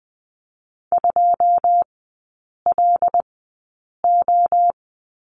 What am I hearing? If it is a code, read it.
Morse code, heard as a keyed tone, about 20 wpm, spaced as for 8 wpm, 700 Hz. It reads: 2LO
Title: 152 Lesson 25e - Letter p - Three Characters 20wpm